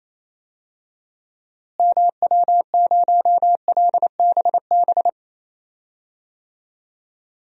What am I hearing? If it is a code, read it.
Morse code, heard as a keyed tone, about 28 wpm, 700 Hz. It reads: MW0LBB
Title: DXCC-100 Call Signs - 6 Characters - Repeated 2x - 28wpm